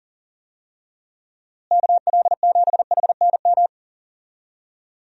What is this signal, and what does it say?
Morse code, heard as a keyed tone, about 40 wpm, 700 Hz. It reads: KL7HNM
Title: USA Call Signs - 6 Characters - Repeated 3x - 40wpm